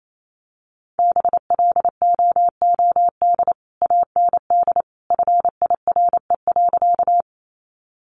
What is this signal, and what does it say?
Morse code, heard as a keyed tone, about 28 wpm, 700 Hz. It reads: BLOOD AND FIRE.
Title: Sentences from Top 1100 Words 28wpm